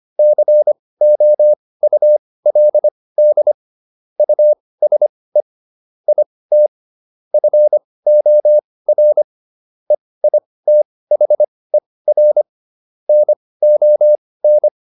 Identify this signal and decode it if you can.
Morse code, heard as a keyed tone, about 25 wpm, spaced as for 18 wpm, 600 Hz. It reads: COULD USE IT FOR EITHER NON